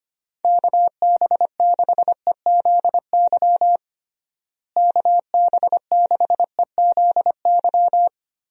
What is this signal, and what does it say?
Morse code, heard as a keyed tone, about 25 wpm, 700 Hz. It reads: KB6EZY KB6EZY
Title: USA Call Signs - 6 Characters - Repeated 3x - 25wpm